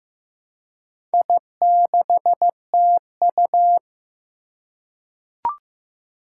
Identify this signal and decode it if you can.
Morse code, heard as a keyed tone, about 15 wpm, 700 Hz. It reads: I6TU
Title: Four Challenging Letter-Number Combinations 15wpm